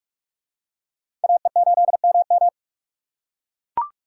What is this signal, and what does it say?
Morse code, heard as a keyed tone, about 42 wpm, 700 Hz. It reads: AE8MM